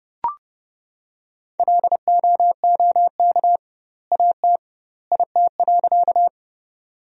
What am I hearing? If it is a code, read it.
Morse code, heard as a keyed tone, about 30 wpm, 700 Hz. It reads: LOOK AT IT.
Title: 2x Word-Spacing - Sentences from Top 100 Words - 30wpm